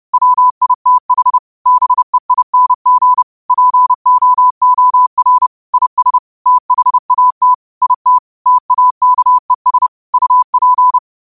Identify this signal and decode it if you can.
Morse code, heard as a keyed tone, about 30 wpm, 1000 Hz. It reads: WITH BEING POOR IS THAT IT TAKES UP
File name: qotdfiles / 30wpm